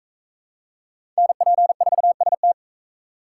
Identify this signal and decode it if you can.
Morse code, heard as a keyed tone, about 38 wpm, 700 Hz. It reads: NP4ST